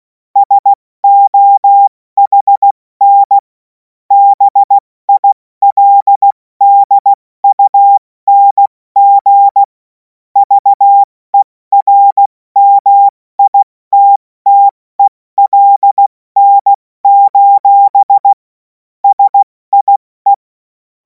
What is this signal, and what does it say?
Morse code, heard as a keyed tone, about 16 wpm, 800 Hz. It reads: SOHN BILDUNG VERMITTELN: SIE
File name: Morsekurs / Witze / Witz 24. März 2026 70 F80